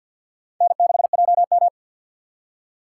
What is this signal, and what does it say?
Morse code, heard as a keyed tone, about 52 wpm, 700 Hz. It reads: N6JM
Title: USA Call Signs - 4 Characters - 3x Character Spacing - 50wpm